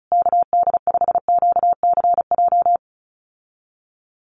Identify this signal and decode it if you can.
Morse code, heard as a keyed tone, about 34 wpm, 700 Hz. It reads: KD5QCJ
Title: USA Call Signs Repeated 3x 35wpm